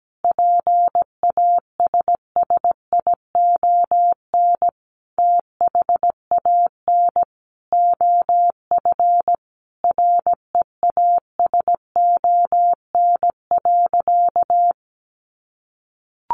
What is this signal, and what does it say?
Morse code, heard as a keyed tone, about 17 wpm, 700 Hz. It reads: PASSION THAN OF REASON.
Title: Sentences from Top 1300 Words 17wpm